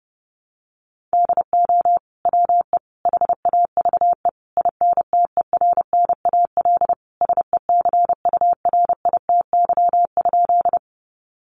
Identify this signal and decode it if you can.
Morse code, heard as a keyed tone, about 30 wpm, 700 Hz. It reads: DO WE HAVE INTERNAL SECURITY?
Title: Sentences from Top 1400 Words 30wpm